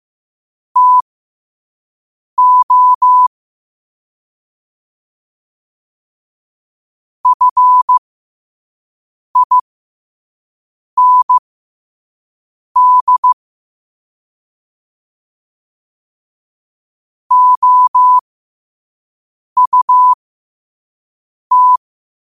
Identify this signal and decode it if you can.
Morse code, heard as a keyed tone, about 15 wpm, spaced as for 5 wpm, 1000 Hz. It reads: TO FIND OUT
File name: qotdfiles / 5wpm